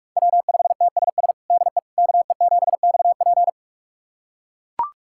Morse code, heard as a keyed tone, about 43 wpm, 700 Hz. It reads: W5TSS DE KE7XP
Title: USA Callsign DE Callsign 45wpm